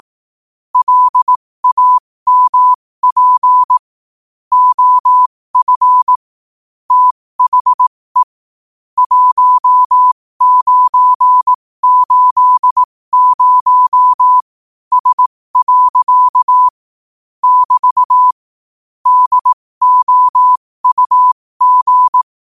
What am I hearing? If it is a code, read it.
Morse code, heard as a keyed tone, about 18 wpm, spaced as for 15 wpm, 1000 Hz. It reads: LAMP OF THE 1980S. <BT> DOUG
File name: qotdfiles / 15wpm